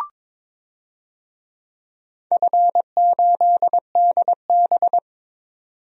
Morse code, heard as a keyed tone, about 22 wpm, 700 Hz. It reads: F8DB
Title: DXCC-100 Call Signs - 4 Characters - 22wpm